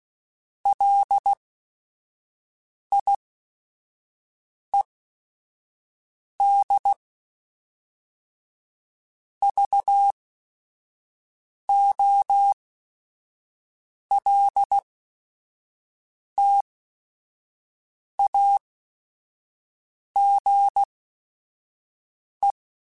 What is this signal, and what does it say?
Morse code, heard as a keyed tone, about 16 wpm, spaced as for 5 wpm, 800 Hz. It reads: LIED VOLTAGE
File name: Technical Subjects / Technical Subjects-mp3s / XC @5